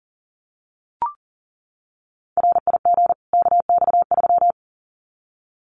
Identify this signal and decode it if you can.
Morse code, heard as a keyed tone, about 40 wpm, 700 Hz. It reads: RIG KX3
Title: Common QSO Phrases 40wpm